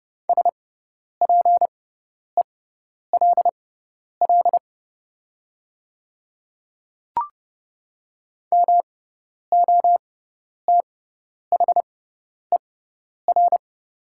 Morse code, heard as a keyed tone, about 30 wpm, spaced as for 10 wpm, 700 Hz. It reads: SPELL MOTHER
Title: 6x Character Spacing - Top 200 Words - 30wpm